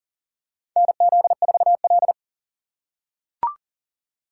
Morse code, heard as a keyed tone, about 42 wpm, 700 Hz. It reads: NZ4L
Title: USA Call Signs - 4 Characters - 3x Character Spacing - 40wpm